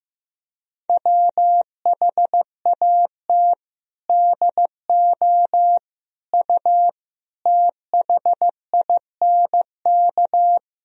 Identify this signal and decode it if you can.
Morse code, heard as a keyed tone, about 15 wpm, 700 Hz. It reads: WHAT DO U THINK
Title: Sentences from Top 200 Words and Abbreviations 15wpm